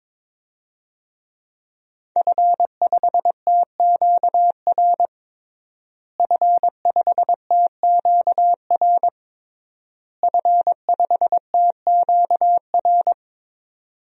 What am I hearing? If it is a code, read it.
Morse code, heard as a keyed tone, about 23 wpm, 700 Hz. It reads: F5TQR F5TQR F5TQR
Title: DXCC-100 Call Signs - 5 Characters - Repeated 3x - 22wpm